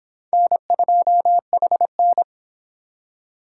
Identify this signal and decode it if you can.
Morse code, heard as a keyed tone, about 26 wpm, 700 Hz. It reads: N2HN